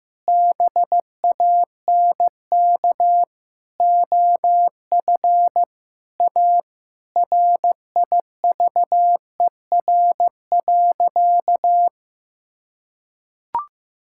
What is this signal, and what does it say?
Morse code, heard as a keyed tone, about 15 wpm, 700 Hz. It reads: BANK OF A RIVER.
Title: Sentences from Top 1000 Words - Review Part 6 - 15wpm